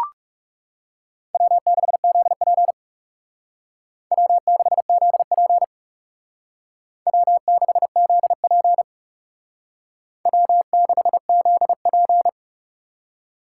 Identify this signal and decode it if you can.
Morse code, heard as a keyed tone, about 36 wpm, 700 Hz. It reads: W6ZP W6ZP W6ZP W6ZP
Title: Speed Racing - USA Call Signs 30wpm